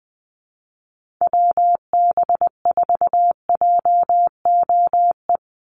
Morse code, heard as a keyed tone, about 20 wpm, 700 Hz. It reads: WB4JOE
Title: USA Call Signs 20wpm